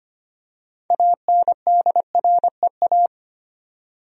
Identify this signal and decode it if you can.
Morse code, heard as a keyed tone, about 25 wpm, 700 Hz. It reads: ANDREA
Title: USA Common Names - 25wpm